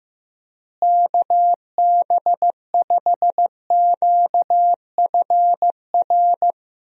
Morse code, heard as a keyed tone, about 15 wpm, 700 Hz. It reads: KB5QFR
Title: USA Call Signs - 6 Characters - Repeated 3x - 15wpm